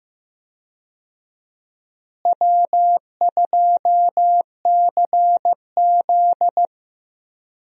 Morse code, heard as a keyed tone, about 15 wpm, 700 Hz. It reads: W2CZ
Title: USA Call Signs - 4 Characters - Repeated 2x - 15wpm